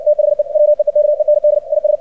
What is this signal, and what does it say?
Morse code, heard as a keyed tone, about 60 wpm, 600 Hz. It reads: THE WEEKEND IS
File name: archived / CRSnet